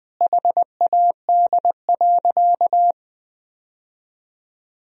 Morse code, heard as a keyed tone, about 21 wpm, 700 Hz. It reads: HAD.
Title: Sentences from Top 1000 Words - Review Part 2 - 20wpm